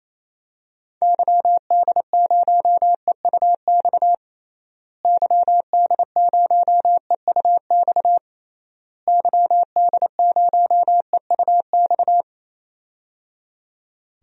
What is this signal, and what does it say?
Morse code, heard as a keyed tone, about 28 wpm, 700 Hz. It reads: YD0EUX YD0EUX YD0EUX
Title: DXCC-100 Call Signs - 6 Characters - Repeated 3x - 28wpm